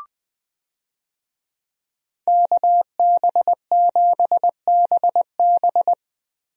Morse code, heard as a keyed tone, about 20 wpm, 700 Hz. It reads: KB7BB
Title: USA Call Signs - 5 Characters - Repeated 2x - 20wpm